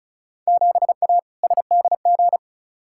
Morse code, heard as a keyed tone, about 36 wpm, 690 Hz. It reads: 7A SDG
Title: Field Day Exchanges Encore - 35wpm